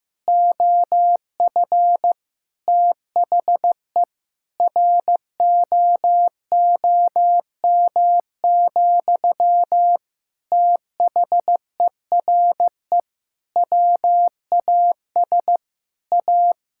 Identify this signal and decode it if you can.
Morse code, heard as a keyed tone, about 15 wpm, 700 Hz. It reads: OF THE ROOM, THERE WAS A
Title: Sentences from Top 1000 Words - Review Part 6 - 15wpm